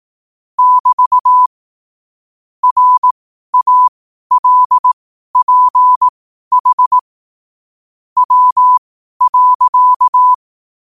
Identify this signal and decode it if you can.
Morse code, heard as a keyed tone, about 18 wpm, spaced as for 12 wpm, 1000 Hz. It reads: <BT> RALPH W.
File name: qotdfiles / 12wpm